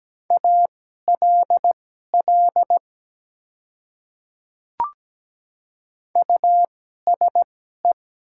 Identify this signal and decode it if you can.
Morse code, heard as a keyed tone, about 17 wpm, spaced as for 12 wpm, 700 Hz. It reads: ALL USE
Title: 2x Character Spacing - Top 100 Words - 17wpm